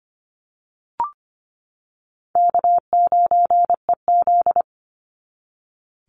Morse code, heard as a keyed tone, about 25 wpm, 700 Hz. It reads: K9EZ
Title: NA SOTA Chaser Callsigns 25wpm